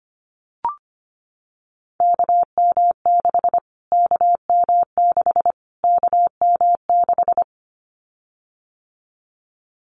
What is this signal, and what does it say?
Morse code, heard as a keyed tone, about 25 wpm, 700 Hz. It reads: KM6 KM6 KM6
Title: Callsign Prefixes for Top 25 DXCC - Repeated 3x - 25wpm